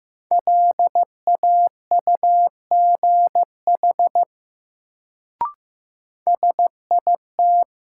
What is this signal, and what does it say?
Morse code, heard as a keyed tone, about 15 wpm, 700 Hz. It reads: LAUGH SIT
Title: Sentences from Top 500 Words - New Word Drill - 15wpm